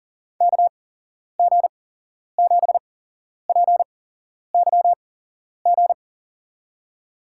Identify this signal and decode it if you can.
Morse code, heard as a keyed tone, about 40 wpm, 700 Hz. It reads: K G 7 P Y G
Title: USA Call Signs - 6 Characters - 8x Character Spacing - 40wpm